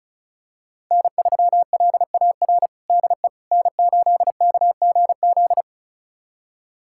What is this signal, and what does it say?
Morse code, heard as a keyed tone, about 36 wpm, 700 Hz. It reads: N3LAR DE N8KGZ
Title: USA Callsign DE Callsign - 5 Characters - 35wpm